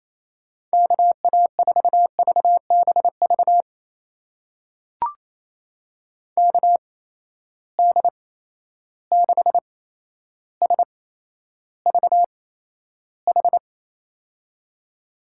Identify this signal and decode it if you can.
Morse code, heard as a keyed tone, about 28 wpm, spaced as for 12 wpm, 700 Hz. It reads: KA4VBV KD6SVH
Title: USA Call Signs - 6 Characters - 8x Character Spacing - 28wpm